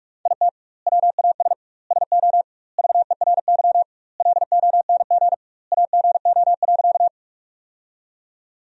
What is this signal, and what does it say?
Morse code, heard as a keyed tone, about 46 wpm, 700 Hz. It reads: IT WAS SO VERY LONG AGO.
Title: 2x Word-Spacing - Sentences from Top 500 Words - 45wpm